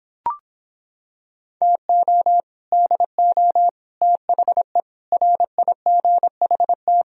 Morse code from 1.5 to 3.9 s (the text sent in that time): TO DO